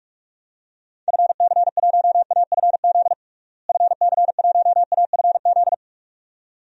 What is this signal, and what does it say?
Morse code, heard as a keyed tone, about 46 wpm, 700 Hz. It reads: FC1AFZ FC1AFZ
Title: DXCC-100 Call Signs - 6 Characters - Repeated 2x - 45wpm